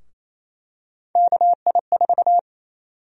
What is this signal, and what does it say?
Morse code, heard as a keyed tone, about 28 wpm, 700 Hz. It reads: KI4
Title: USA Call Sign Prefixes - 28wpm